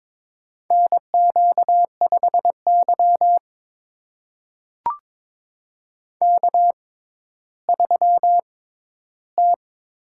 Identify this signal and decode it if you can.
Morse code, heard as a keyed tone, about 22 wpm, spaced as for 12 wpm, 700 Hz. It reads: NQ5Y K3T
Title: USA Call Signs - 6x Character Spacing - 22wpm